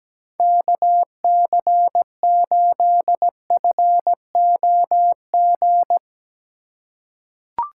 Morse code, heard as a keyed tone, about 17 wpm, 700 Hz. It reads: KC8FOG
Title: USA Call Signs - 6 Characters - 2x Character Spacing - 17wpm